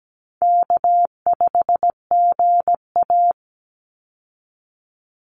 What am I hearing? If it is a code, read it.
Morse code, heard as a keyed tone, about 17 wpm, 700 Hz. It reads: K5GA
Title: Easy Call Signs Repeated 2x - 17wpm